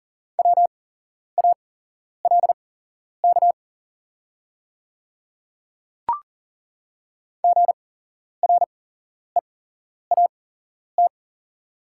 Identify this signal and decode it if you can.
Morse code, heard as a keyed tone, about 40 wpm, spaced as for 11 wpm, 700 Hz. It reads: WALK GREAT
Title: 8x Character Spacing - Top 500 Words - 40wpm